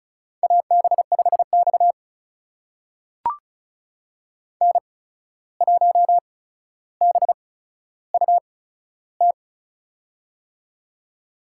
Morse code, heard as a keyed tone, about 34 wpm, spaced as for 14 wpm, 700 Hz. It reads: AB5X N1BUT
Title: USA Call Signs - 8x Character Spacing - 35wpm